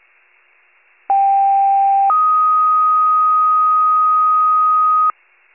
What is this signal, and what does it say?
Two-tone page, heard as a one-second tone, then a three-second tone, about 790 then 1250 Hz. It is heard from 1.1 to 5.1 s.
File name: pager / 73 RESCUE / wcalert